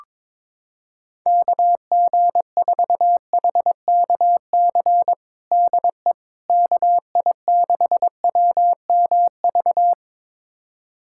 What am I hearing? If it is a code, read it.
Morse code, heard as a keyed tone, about 22 wpm, 700 Hz. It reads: KG4HKC DE KI6WMV
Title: USA Callsign DE Callsign 22wpm